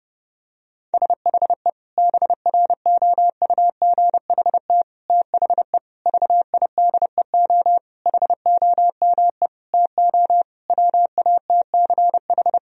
Morse code, heard as a keyed tone, about 30 wpm, 700 Hz. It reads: SHE BROUGHT THE VIDEO HOME TO WATCH
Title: Sentences from Top 1000 Words - Review Part 4 - 30wpm